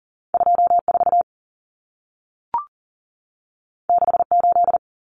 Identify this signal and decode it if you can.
Morse code, heard as a keyed tone, about 40 wpm, 700 Hz. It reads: 24 68
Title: Numbers - 2-Digit 40wpm